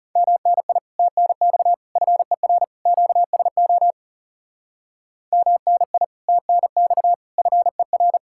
Morse code, heard as a keyed tone, about 36 wpm, 700 Hz. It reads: MNI TNX FER QSO MNI TNX FER
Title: Speed-Racing Common QSO Phrases Encore - 30wpm